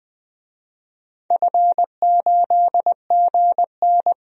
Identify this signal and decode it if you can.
Morse code, heard as a keyed tone, about 20 wpm, 700 Hz. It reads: F8GN
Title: DXCC-100 Call Signs - 4 Characters - 20wpm